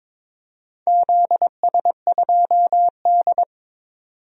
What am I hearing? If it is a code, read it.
Morse code, heard as a keyed tone, about 22 wpm, 700 Hz. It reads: ZS2D
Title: DXCC-100 Call Signs - 4 Characters - 22wpm